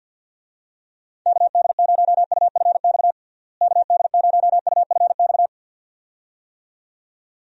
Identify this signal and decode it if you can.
Morse code, heard as a keyed tone, about 48 wpm, 700 Hz. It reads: KD0UFX KD0UFX
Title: USA Call Signs - 6 Characters - Repeated 2x - 50wpm